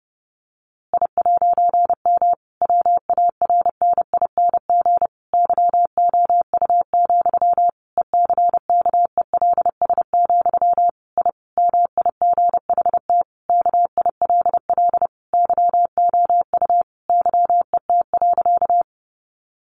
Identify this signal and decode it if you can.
Morse code, heard as a keyed tone, about 30 wpm, 700 Hz. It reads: I'M WARNING YOU, ECKELS, I MIGHT KILL YOU YET.
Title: A Sound Of Thunder 30wpm